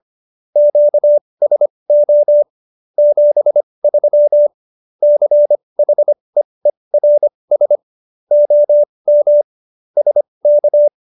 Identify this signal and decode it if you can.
Morse code, heard as a keyed tone, about 25 wpm, spaced as for 20 wpm, 590 Hz. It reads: QSO 73 CHEERS OM SK
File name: QSO310 20